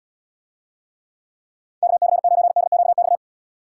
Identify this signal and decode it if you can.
Morse code, heard as a keyed tone, about 78 wpm, 700 Hz. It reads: KC9DQX